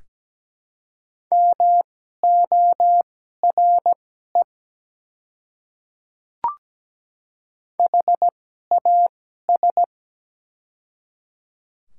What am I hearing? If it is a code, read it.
Morse code, heard as a keyed tone, about 17 wpm, spaced as for 12 wpm, 700 Hz. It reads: MORE HAS
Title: 2x Character Spacing - Top 100 Words - 17wpm